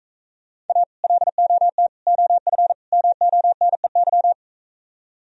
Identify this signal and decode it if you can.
Morse code, heard as a keyed tone, about 45 wpm, 700 Hz. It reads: A LOT OF MONEY